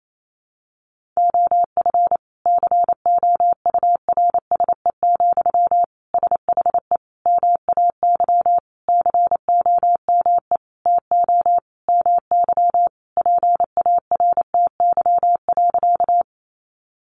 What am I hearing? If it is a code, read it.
Morse code, heard as a keyed tone, about 28 wpm, 700 Hz. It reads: OF COURSE, SHE MAY COME TO MY PARTY.
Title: Sentences from Top 1400 Words 28wpm